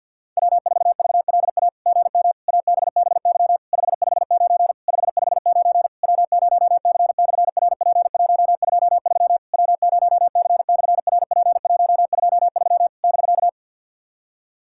Morse code, heard as a keyed tone, about 48 wpm, 690 Hz. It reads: W4VLA GM ABBY 559 559 W0C/RP123 W0C/RP123 <BK>
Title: SOTA QSO Practice 50wpm